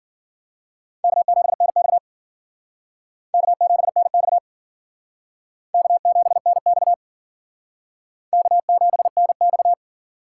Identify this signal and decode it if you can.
Morse code, heard as a keyed tone, about 50 wpm, 700 Hz. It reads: K7NX K7NX K7NX K7NX
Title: Speed Racing - USA Call Signs 40wpm